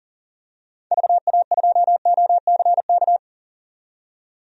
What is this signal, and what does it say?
Morse code, heard as a keyed tone, about 40 wpm, 700 Hz. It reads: VA2OCK
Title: DXCC-100 Call Signs - 6 Characters - 40wpm